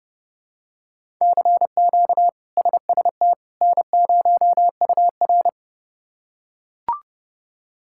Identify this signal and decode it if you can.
Morse code, heard as a keyed tone, about 32 wpm, 700 Hz. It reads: CQ SST N0UR
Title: K1USN SST QSO Practice - 30wpm